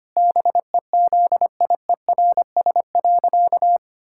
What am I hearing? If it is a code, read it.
Morse code, heard as a keyed tone, about 25 wpm, 700 Hz. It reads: BEZIERS.
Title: Pangram Challenge v2 - 25wpm